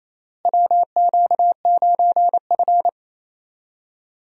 Morse code, heard as a keyed tone, about 28 wpm, 700 Hz. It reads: WQ9F
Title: USA Call Signs - 4 Characters - Repeated 2x - 28wpm